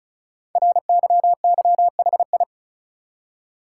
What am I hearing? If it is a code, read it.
Morse code, heard as a keyed tone, about 36 wpm, 690 Hz. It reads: RYYHI
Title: Code Groups - 5 Letters - 35wpm